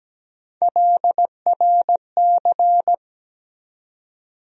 Morse code, heard as a keyed tone, about 17 wpm, 700 Hz. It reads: LRC
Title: USA Call Sign Postfixes - 3 Letters 17wpm